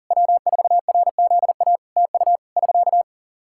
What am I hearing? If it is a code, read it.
Morse code, heard as a keyed tone, about 37 wpm, 700 Hz. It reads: W4RZA TU <SK>
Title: Practice QSO - Exchange 4 - Ending - 40wpm